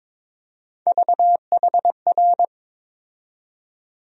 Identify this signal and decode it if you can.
Morse code, heard as a keyed tone, about 21 wpm, 700 Hz. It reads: VHR
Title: USA Call Sign Postfixes - 3 Letters 22wpm